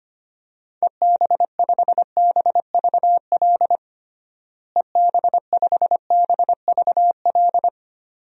Morse code, heard as a keyed tone, about 25 wpm, 700 Hz. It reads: EB5BVL EB5BVL
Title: DXCC-100 Call Signs - 6 Characters - Repeated 2x - 25wpm